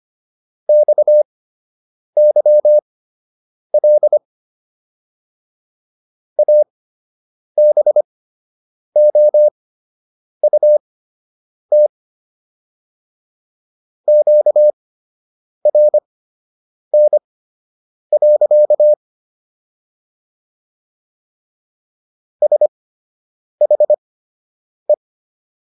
Morse code, heard as a keyed tone, about 25 wpm, spaced as for 8 wpm, 600 Hz. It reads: XYL ABOUT QRN. SHE